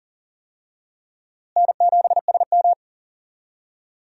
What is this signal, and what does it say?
Morse code, heard as a keyed tone, about 39 wpm, 700 Hz. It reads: N7SM
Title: USA Call Signs - 4 Characters - Repeated 2x - 40wpm